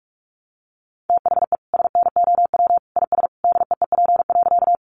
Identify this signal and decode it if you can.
Morse code, heard as a keyed tone, about 44 wpm, 700 Hz. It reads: THE SNOW IS DEEP.
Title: Sentences from Top 500 Words 45wpm